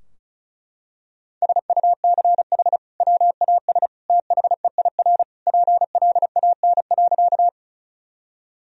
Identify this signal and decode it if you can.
Morse code, heard as a keyed tone, about 34 wpm, 700 Hz. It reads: SUCH WAS THEIR PLAN.
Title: Sentences from Top 1000 Words - Review Part 2 - 35wpm